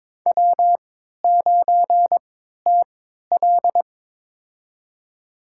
Morse code, heard as a keyed tone, about 21 wpm, spaced as for 12 wpm, 700 Hz. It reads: W9TL
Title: USA Call Signs - 3x Character Spacing - 22wpm